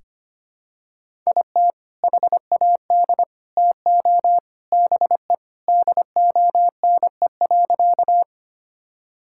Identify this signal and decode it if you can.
Morse code, heard as a keyed tone, about 25 wpm, 700 Hz. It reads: IT HAD TO BE DONE.
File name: practice / 25wpm / Sentences with 3-Word Phrases - 25wpm